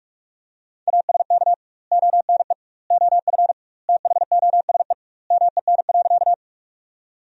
Morse code, heard as a keyed tone, about 43 wpm, 700 Hz. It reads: ASK ONE OF THOSE MEN.
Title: 2x Word-Spacing - Sentences from Top 300 Words - 45wpm